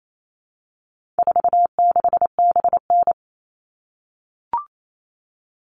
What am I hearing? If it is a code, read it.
Morse code, heard as a keyed tone, about 28 wpm, 700 Hz. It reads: 46BN
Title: All Cut Numbers - 4-Digit 28wpm